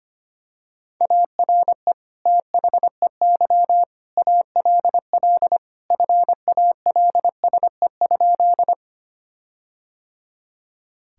Morse code, heard as a keyed tone, about 25 wpm, 700 Hz. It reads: ARE THEY ALL FALSE?
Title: Sentences from Top 1600 Words 25wpm